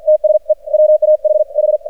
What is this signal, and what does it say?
Morse code, heard as a keyed tone, about 48 wpm, spaced as for 41 wpm, 610 Hz. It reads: TSE WAHH
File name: archived / CRSnet